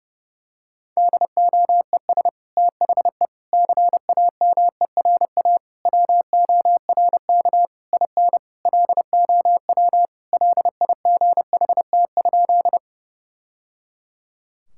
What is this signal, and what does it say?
Morse code, heard as a keyed tone, about 32 wpm, 700 Hz. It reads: DOES THE CAMERA WORK IN LOW LIGHT?
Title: Sentences from Top 1000 Words - Review Part 5 - 30wpm